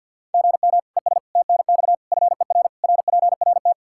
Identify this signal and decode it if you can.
Morse code, heard as a keyed tone, about 46 wpm, 700 Hz. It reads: GM ES TNX FER RPRT
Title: K1USN SST QSO Practice 50wpm